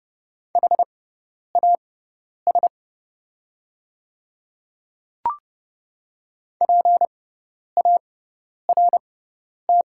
Morse code, heard as a keyed tone, about 30 wpm, spaced as for 10 wpm, 700 Hz. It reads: HAS PART
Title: 6x Character Spacing - Top 200 Words - 30wpm